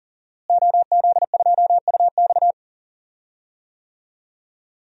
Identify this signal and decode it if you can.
Morse code, heard as a keyed tone, about 40 wpm, 700 Hz. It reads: OZ2UX
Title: DXCC-100 Call Signs - 5 Characters - Repeated 3x - 40wpm